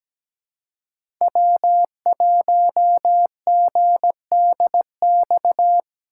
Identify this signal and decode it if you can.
Morse code, heard as a keyed tone, about 20 wpm, 700 Hz. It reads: W1GDX
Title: Speed Racing - USA 1x3 Call Signs 17wpm